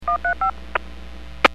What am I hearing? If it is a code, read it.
DTMF, heard as telephone touch tones, tones of about 110 ms. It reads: 135
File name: 451.25 WPS